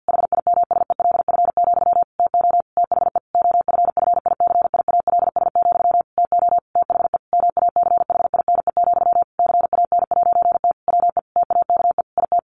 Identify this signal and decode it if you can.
Morse code, heard as a keyed tone, about 50 wpm, 700 Hz. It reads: HIMSELF, TO THE OFFICIALS, TO THE MACHINE, CAN'T WE TAKE IT
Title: A Sound Of Thunder 50wpm